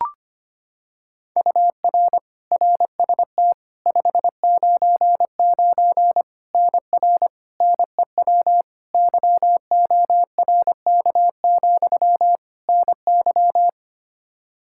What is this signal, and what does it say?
Morse code, heard as a keyed tone, about 25 wpm, 700 Hz. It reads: UR RST 599 NR NEW YORK, NY
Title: K1USN SST QSO Practice 25wpm